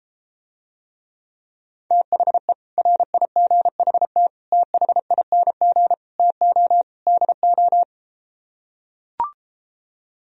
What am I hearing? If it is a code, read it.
Morse code, heard as a keyed tone, about 33 wpm, 700 Hz. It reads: THE RIGHT THING TO DO